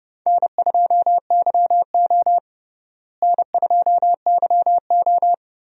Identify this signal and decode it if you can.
Morse code, heard as a keyed tone, about 30 wpm, 700 Hz. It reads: N2YO N2YO
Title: USA Call Signs - 4 Characters - Repeated 3x - 30wpm